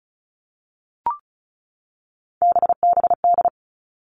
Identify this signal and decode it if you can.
Morse code, heard as a keyed tone, about 34 wpm, 700 Hz. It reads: BBD
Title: All Cut Numbers - 3-Digit 35wpm